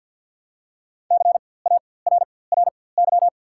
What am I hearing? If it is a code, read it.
Morse code, heard as a keyed tone, about 54 wpm, spaced as for 24 wpm, 690 Hz. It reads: CARRY
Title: 4x Character Spacing - Top 500 Words - 50wpm